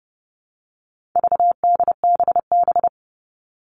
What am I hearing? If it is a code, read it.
Morse code, heard as a keyed tone, about 30 wpm, 700 Hz. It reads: VDBB
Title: All Cut Numbers - 4-Digit 30wpm